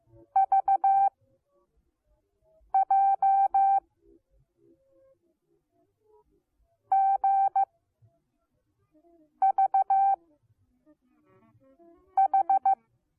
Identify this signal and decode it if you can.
Morse code, heard as a keyed tone, about 15 wpm, 780 Hz. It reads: V J G V H